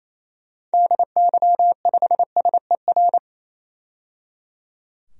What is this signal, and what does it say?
Morse code, heard as a keyed tone, about 27 wpm, 700 Hz. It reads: DY5SER
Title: DXCC-100 Call Signs - 6 Characters - 28wpm